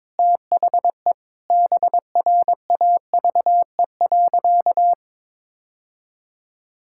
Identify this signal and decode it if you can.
Morse code, heard as a keyed tone, about 21 wpm, 700 Hz. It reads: THE BRAVE.
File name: Sentences with 9-Word Phrases - 22wpm